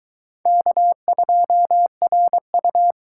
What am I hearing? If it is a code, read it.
Morse code, heard as a keyed tone, about 22 wpm, 700 Hz. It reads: K2RU